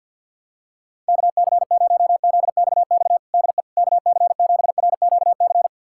Morse code, heard as a keyed tone, about 51 wpm, 700 Hz. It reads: KC0ZXX DE KC7RQC
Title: USA Callsign DE Callsign 50wpm